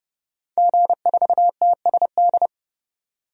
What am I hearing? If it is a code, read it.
Morse code, heard as a keyed tone, about 30 wpm, 700 Hz. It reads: G4TSD
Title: DXCC-100 Call Signs - 5 Characters - 30wpm